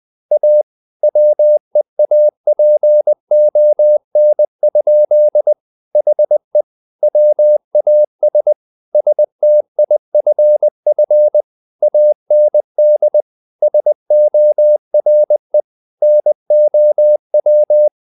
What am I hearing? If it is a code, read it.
Morse code, heard as a keyed tone, about 20 wpm, 600 Hz. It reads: A WEAPON? HE WAS STIFF AND SORE NOW
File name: the old man and the sea part 2 20wpm